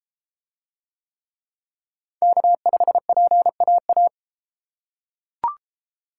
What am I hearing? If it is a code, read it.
Morse code, heard as a keyed tone, about 31 wpm, 700 Hz. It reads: K5PAA